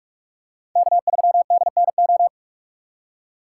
Morse code, heard as a keyed tone, about 46 wpm, 700 Hz. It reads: K3DNO